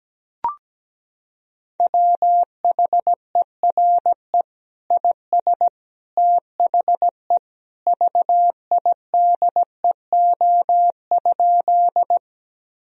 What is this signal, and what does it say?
Morse code, heard as a keyed tone, about 17 wpm, 700 Hz. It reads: WHERE IS THE VIDEO?
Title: Sentences from Top 1000 Words - Review Part 4 - 17wpm